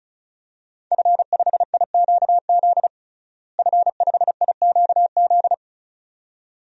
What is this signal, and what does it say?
Morse code, heard as a keyed tone, about 34 wpm, 700 Hz. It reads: F5IQZ F5IQZ
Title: DXCC-100 Call Signs - 5 Characters - Repeated 2x - 35wpm